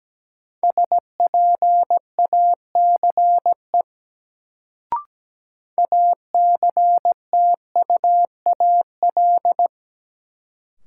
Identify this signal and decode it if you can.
Morse code, heard as a keyed tone, about 17 wpm, 700 Hz. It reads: SPACE ACTUAL
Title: Sentences from Top 1000 Words - Review Part 5 - New Word Drill - 17wpm